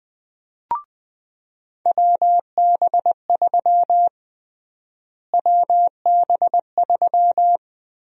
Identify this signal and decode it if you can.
Morse code, heard as a keyed tone, about 20 wpm, 700 Hz. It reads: WB3 WB3
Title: USA Call Sign Prefixes - Repeated 2x - 20wpm